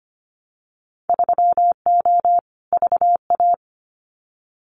Morse code, heard as a keyed tone, about 25 wpm, 700 Hz. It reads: 3O VA
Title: Winter Field Day Exchanges 25wpm